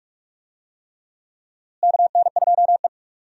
Morse code, heard as a keyed tone, about 42 wpm, 700 Hz. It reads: KN2E
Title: USA Call Signs - 4 Characters - Repeated 3x - 45wpm